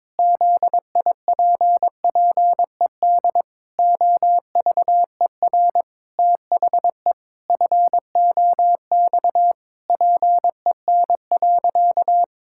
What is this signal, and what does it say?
Morse code, heard as a keyed tone, about 22 wpm, 700 Hz. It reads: ZIPPED OVER THE FOX PEN.
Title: Pangram Challenge v2 - 22wpm